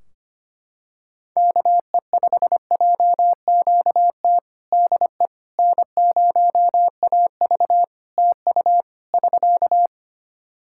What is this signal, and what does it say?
Morse code, heard as a keyed tone, about 25 wpm, 700 Hz. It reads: KE5JQT DE N0AV TU <SK>
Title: Practice QSO - Exchange 4 - Ending - 25wpm